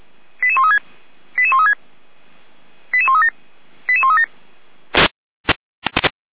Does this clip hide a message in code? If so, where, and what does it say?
Yes: five-tone selective call (ZVEI) from 0.4 to 4.3 s, heard as quick runs of five tones, tones of about 70 ms. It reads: 88136 88136 88137 88137